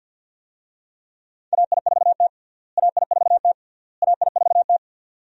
Morse code, heard as a keyed tone, about 52 wpm, 700 Hz. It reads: AI4T AI4T AI4T
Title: USA Call Signs - 4 Characters - Repeated 3x - 50wpm